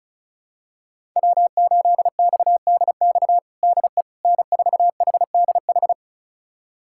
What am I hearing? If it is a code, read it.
Morse code, heard as a keyed tone, about 36 wpm, 700 Hz. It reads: W8XDX DE N4HDH
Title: USA Callsign DE Callsign - 5 Characters - 35wpm